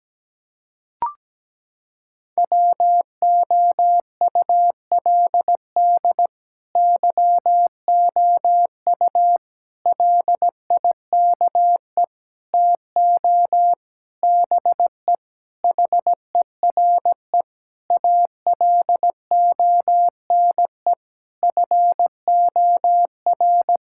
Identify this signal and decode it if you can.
Morse code, heard as a keyed tone, about 17 wpm, 700 Hz. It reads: WOULD YOU LIKE TO BE HERE ALONE FOR
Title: Sentences from Top 1000 Words - Review Part 5 - 17wpm